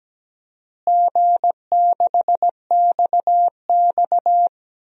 Morse code, heard as a keyed tone, about 17 wpm, 700 Hz. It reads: G6XX
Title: DXCC-100 Call Signs - 4 Characters - Repeated 2x - 17wpm